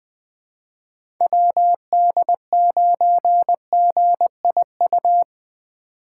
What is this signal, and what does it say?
Morse code, heard as a keyed tone, about 19 wpm, 700 Hz. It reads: WD9GIU